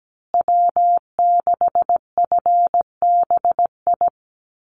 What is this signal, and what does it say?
Morse code, heard as a keyed tone, about 17 wpm, 700 Hz. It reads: W6FBI
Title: USA Call Signs Repeated 2x 17wpm